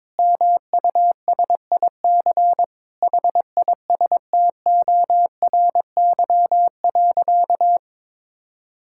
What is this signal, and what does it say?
Morse code, heard as a keyed tone, about 21 wpm, 700 Hz. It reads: MUSIC HISTORY.
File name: Sentences with 7-Word Phrases - 22wpm